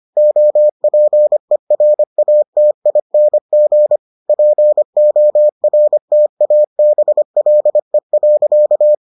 Morse code, heard as a keyed tone, about 25 wpm, 600 Hz. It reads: OPERATING PORTABLE.